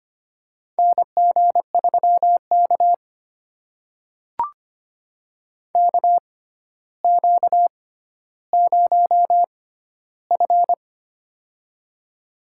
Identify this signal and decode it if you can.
Morse code, heard as a keyed tone, about 25 wpm, spaced as for 13 wpm, 700 Hz. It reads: NG3K KQ0F
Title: USA Call Signs - 4 Characters - 6x Character Spacing - 25wpm